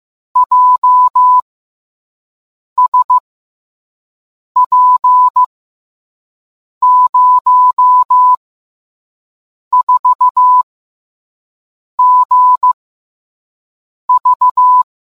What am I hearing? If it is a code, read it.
Morse code, heard as a keyed tone, about 15 wpm, 1000 Hz. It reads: J S P 0 4 G V